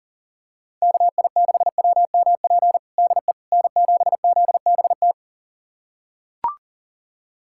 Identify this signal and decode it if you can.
Morse code, heard as a keyed tone, about 40 wpm, 700 Hz. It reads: KI6WMP DE N7ZBT
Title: USA Callsign DE Callsign 40wpm